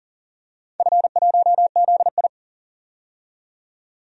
Morse code, heard as a keyed tone, about 38 wpm, 700 Hz. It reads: F1ZI
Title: DXCC-100 Call Signs - 4 Characters - 40wpm